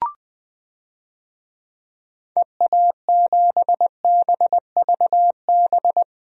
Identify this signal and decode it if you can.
Morse code, heard as a keyed tone, about 20 wpm, 700 Hz. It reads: EA7BVB
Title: DXCC-100 Call Signs - 6 Characters - Repeated 2x - 20wpm